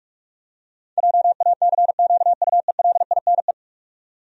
Morse code, heard as a keyed tone, about 46 wpm, 700 Hz. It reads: JACQUELINE
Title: USA Common Names - 45wpm